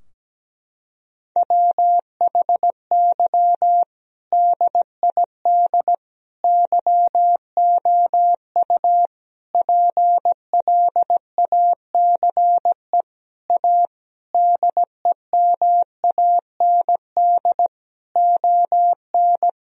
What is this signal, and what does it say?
Morse code, heard as a keyed tone, about 17 wpm, 700 Hz. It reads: WHY DID YOU PLACE A DEMAND ON
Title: Sentences from Top 1000 Words - Review Part 4 - 17wpm